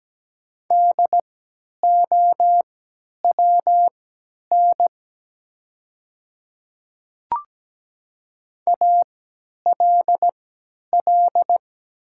Morse code, heard as a keyed tone, about 17 wpm, spaced as for 10 wpm, 700 Hz. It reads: DOWN ALL
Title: 3x Character Spacing - Top 100 Words - 17wpm